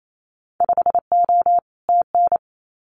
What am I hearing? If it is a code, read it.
Morse code, heard as a keyed tone, about 28 wpm, 700 Hz. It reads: 5O TN
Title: Winter Field Day Exchanges Encore 28wpm